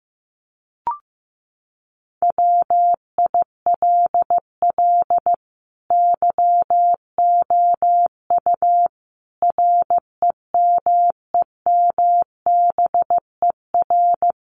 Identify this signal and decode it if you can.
Morse code, heard as a keyed tone, about 15 wpm, 700 Hz. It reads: WILL YOU REMEMBER
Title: Sentences from Top 1100 Words 15wpm